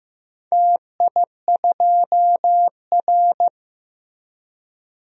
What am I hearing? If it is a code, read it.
Morse code, heard as a keyed tone, about 15 wpm, 700 Hz. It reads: TI2R
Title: DXCC-100 Call Signs - 4 Characters - Repeated 2x - 15wpm